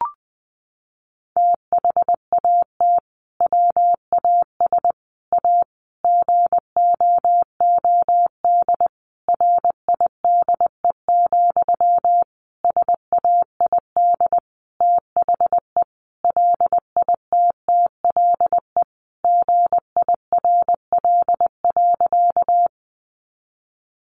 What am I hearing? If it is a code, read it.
Morse code, heard as a keyed tone, about 20 wpm, 700 Hz. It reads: THAT WAS A GOOD RIDE, SAID THE LITTLE GIRL.
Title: Sentences from Top 1200 Words 20wpm